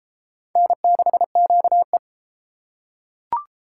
Morse code, heard as a keyed tone, about 28 wpm, 700 Hz. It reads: N6QE